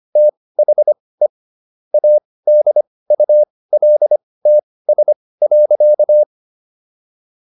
Morse code, heard as a keyed tone, about 25 wpm, spaced as for 18 wpm, 600 Hz. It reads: THE ADULTS.